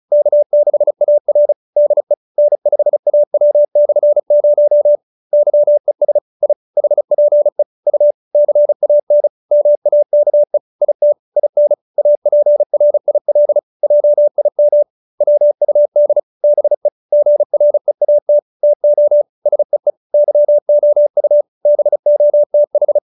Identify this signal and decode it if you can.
Morse code, heard as a keyed tone, about 34 wpm, 590 Hz. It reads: K6AR DE N5AW/0 YES I HPE U CAN MAKE IT IN APRIL JIM WUD BE GREAT TO SEE YOU BOTH
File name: QSO312 35